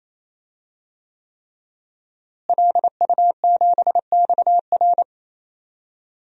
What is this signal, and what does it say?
Morse code, heard as a keyed tone, about 28 wpm, 700 Hz. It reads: LU7XR
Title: DXCC-100 Call Signs - 5 Characters - Repeated 3x - 28wpm